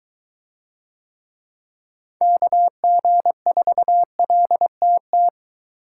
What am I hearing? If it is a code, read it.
Morse code, heard as a keyed tone, about 23 wpm, 700 Hz. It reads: KG4LTT